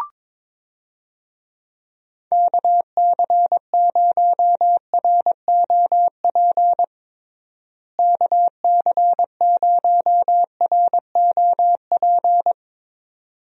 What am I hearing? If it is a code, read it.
Morse code, heard as a keyed tone, about 21 wpm, 700 Hz. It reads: KC0ROP KC0ROP
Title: USA Call Signs - 6 Characters - Repeated 2x - 22wpm